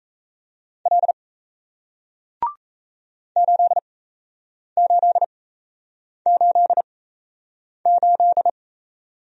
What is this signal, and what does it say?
Morse code, heard as a keyed tone, about 38 wpm, spaced as for 8 wpm, 700 Hz. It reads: L 8888